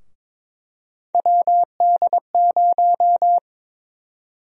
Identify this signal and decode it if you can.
Morse code, heard as a keyed tone, about 22 wpm, 700 Hz. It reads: WD0
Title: USA Call Sign Prefixes - Repeated 2x - 22wpm